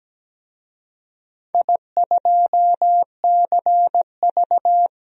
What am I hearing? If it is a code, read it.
Morse code, heard as a keyed tone, about 17 wpm, 700 Hz. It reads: I2CV
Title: DXCC-100 Call Signs - 4 Characters - 17wpm